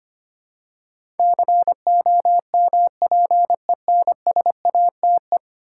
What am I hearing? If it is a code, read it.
Morse code, heard as a keyed tone, about 25 wpm, 700 Hz. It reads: COMPENSATE
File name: 10 Letter Words - First 3 Letters Given - 25wpm